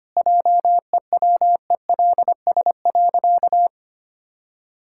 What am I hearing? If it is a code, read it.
Morse code, heard as a keyed tone, about 25 wpm, 700 Hz. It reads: JEWELS.
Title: Pangram Challenge v2 - 25wpm